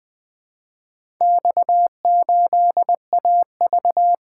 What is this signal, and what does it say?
Morse code, heard as a keyed tone, about 20 wpm, 700 Hz. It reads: X8AV
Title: Four Letter-Number Combinations - 20wpm